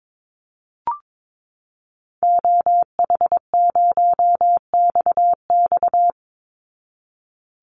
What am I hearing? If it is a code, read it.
Morse code, heard as a keyed tone, about 22 wpm, 700 Hz. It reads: OH0XX
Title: Call Signs Repeated 3x - 22wpm